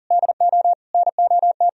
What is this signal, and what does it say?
Morse code, heard as a keyed tone, about 52 wpm, 700 Hz. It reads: DO NOT